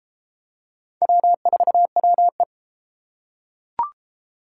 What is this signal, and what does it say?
Morse code, heard as a keyed tone, about 32 wpm, 700 Hz. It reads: W4WE